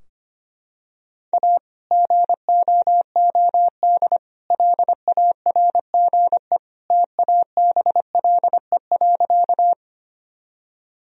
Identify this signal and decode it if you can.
Morse code, heard as a keyed tone, about 25 wpm, 700 Hz. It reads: A GOOD LARGE TABLE.
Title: Sentences from Top 1000 Words - Review Part 2 - 25wpm